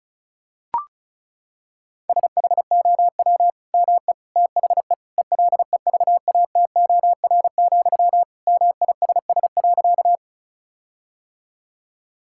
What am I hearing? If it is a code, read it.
Morse code, heard as a keyed tone, about 34 wpm, 700 Hz. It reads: SHOW ME THE ELEVATOR, MISS.
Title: Sentences from Top 1600 Words 35wpm